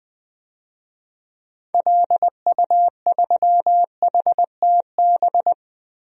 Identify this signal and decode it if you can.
Morse code, heard as a keyed tone, about 20 wpm, 700 Hz. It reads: LU3HTB
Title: DXCC-100 Call Signs - 6 Characters - Repeated 3x - 20wpm